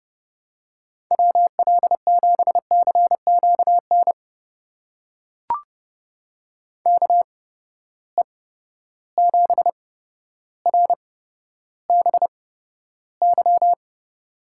Morse code, heard as a keyed tone, about 30 wpm, spaced as for 13 wpm, 700 Hz. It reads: WL7CQN KE7RBY
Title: USA Call Signs - 6 Characters - 8x Character Spacing - 30wpm